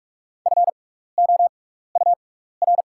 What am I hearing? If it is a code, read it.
Morse code, heard as a keyed tone, about 44 wpm, spaced as for 15 wpm, 690 Hz. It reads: FOUR
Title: 6x Character Spacing - Top 300 Words - 45wpm